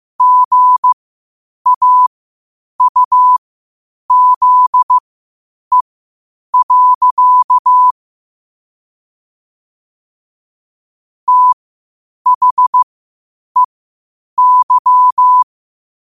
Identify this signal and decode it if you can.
Morse code, heard as a keyed tone, about 15 wpm, spaced as for 8 wpm, 1000 Hz. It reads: GAUZE. THEY
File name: qotdfiles / 8wpm